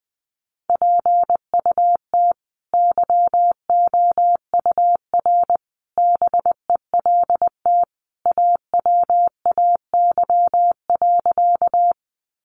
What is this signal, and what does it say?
Morse code, heard as a keyed tone, about 19 wpm, 700 Hz. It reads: PUT YOUR BELT AWAY.
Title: Sentences from Top 1400 Words 20wpm